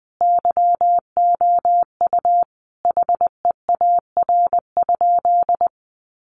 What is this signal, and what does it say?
Morse code, heard as a keyed tone, about 20 wpm, 700 Hz. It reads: YOU HEAR?
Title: Sentences from Top 300 Words 20wpm